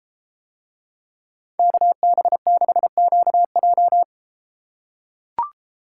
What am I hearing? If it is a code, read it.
Morse code, heard as a keyed tone, about 32 wpm, 700 Hz. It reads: KB6QJ